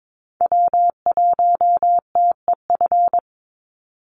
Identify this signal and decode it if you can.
Morse code, heard as a keyed tone, about 22 wpm, 700 Hz. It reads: W1TEF
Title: Call Signs Repeated 2x - 22wpm